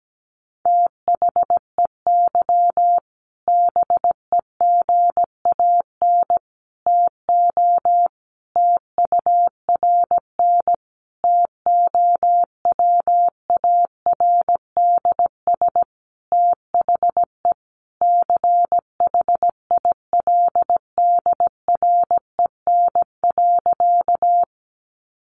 Morse code, heard as a keyed tone, about 17 wpm, 700 Hz. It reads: THEY BEGAN TO TURN TOWARDS THE CHILDREN.
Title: Sentences from Top 1100 Words 17wpm